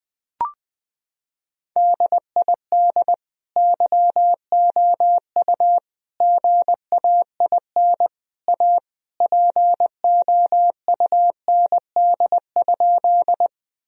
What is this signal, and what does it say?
Morse code, heard as a keyed tone, about 20 wpm, 700 Hz. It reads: DID YOU GAIN A POUND?
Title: Sentences from Top 1000 Words - Review Part 2 - 20wpm